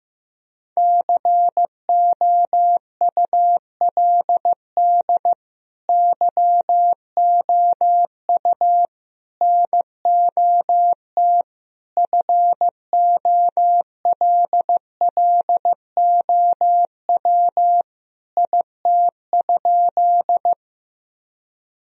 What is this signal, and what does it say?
Morse code, heard as a keyed tone, about 15 wpm, 700 Hz. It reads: COULD YOU NOT FOLLOW IT?
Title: Sentences from Top 1000 Words - Review Part 1 - 15wpm